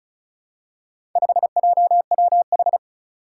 Morse code, heard as a keyed tone, about 36 wpm, 700 Hz. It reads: 5JWH
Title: Four Challenging Letter-Number Combinations 35wpm